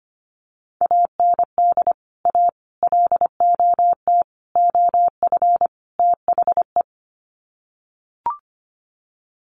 Morse code, heard as a keyed tone, about 25 wpm, 700 Hz. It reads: AND A LOT OF THE
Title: Sets of 5 Words Encore 25wpm